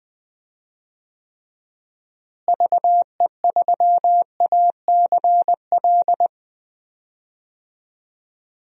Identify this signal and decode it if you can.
Morse code, heard as a keyed tone, about 20 wpm, 700 Hz. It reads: VE3ACL
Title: DXCC-100 Call Signs - 6 Characters - Repeated 3x - 20wpm